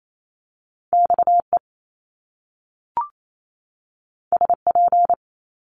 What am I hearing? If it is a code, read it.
Morse code, heard as a keyed tone, about 28 wpm, 700 Hz. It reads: XE SP
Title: Top Two Letter Combinations in Words 28wpm